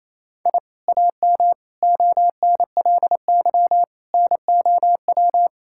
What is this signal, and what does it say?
Morse code, heard as a keyed tone, about 28 wpm, 700 Hz. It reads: I AM ONLY NOW
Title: Sentences from Top 1000 Words - Review Part 5 - 28wpm